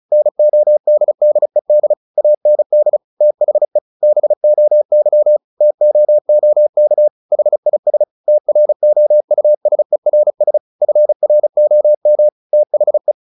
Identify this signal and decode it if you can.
Morse code, heard as a keyed tone, about 34 wpm, 600 Hz. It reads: NODDED AND THE BOY TOOK HIS TROUSERS FROM THE
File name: the old man and the sea / the old man and the sea 1 35wpm